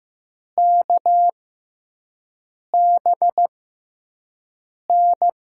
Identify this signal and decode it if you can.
Morse code, heard as a keyed tone, about 15 wpm, spaced as for 5 wpm, 700 Hz. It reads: KBN
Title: USA Call Signs - 6 Characters - 6x Character Spacing - 15wpm